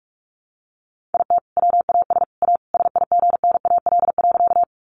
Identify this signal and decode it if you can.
Morse code, heard as a keyed tone, about 44 wpm, 700 Hz. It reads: IT WAS A SIGNAL.
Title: Sentences from Top 1400 Words 45wpm